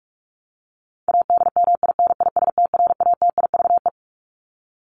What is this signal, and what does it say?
Morse code, heard as a keyed tone, about 44 wpm, 700 Hz. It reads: ADMINISTRATIVE
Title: Top 1501-1750 Words 45wpm